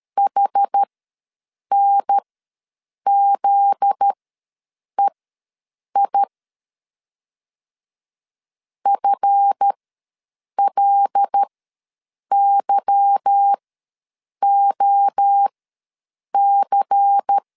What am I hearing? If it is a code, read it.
Morse code, heard as a keyed tone, about 13 wpm, spaced as for 7 wpm, 800 Hz. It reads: HNZEI FLYOC
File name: LEZIONI / 8 YJ45 / 8 Y ws 12wpm 3409